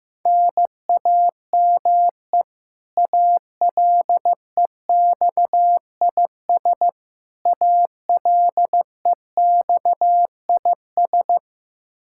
Morse code, heard as a keyed tone, about 15 wpm, 700 Hz. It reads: NAME ALEXIS ALEXIS
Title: Practice QSO - Exchange 1 - Beginning - 15wpm